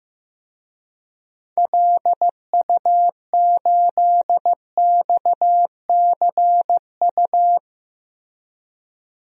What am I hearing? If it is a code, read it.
Morse code, heard as a keyed tone, about 15 wpm, 700 Hz. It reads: LU8XCU
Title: DXCC-100 Call Signs - 6 Characters - Repeated 2x - 15wpm